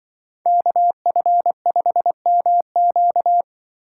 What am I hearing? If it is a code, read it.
Morse code, heard as a keyed tone, about 25 wpm, 700 Hz. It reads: KF5MQ
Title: Speed Racing - USA 2x2 Call Signs 20wpm